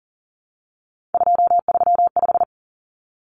Morse code, heard as a keyed tone, about 40 wpm, 700 Hz. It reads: 235
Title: Common Cut Numbers - 3-Digit 40wpm